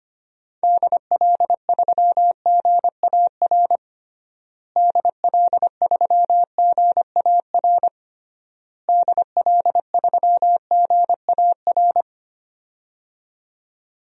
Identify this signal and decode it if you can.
Morse code, heard as a keyed tone, about 25 wpm, 700 Hz. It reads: DL3GAR DL3GAR DL3GAR
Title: DXCC-100 Call Signs - 6 Characters - Repeated 3x - 25wpm